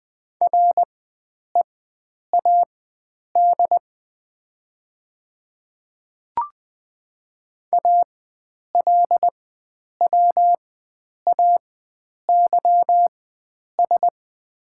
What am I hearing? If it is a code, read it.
Morse code, heard as a keyed tone, about 21 wpm, spaced as for 9 wpm, 700 Hz. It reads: READ ALWAYS
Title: 4x Character Spacing - Top 300 Words - 20wpm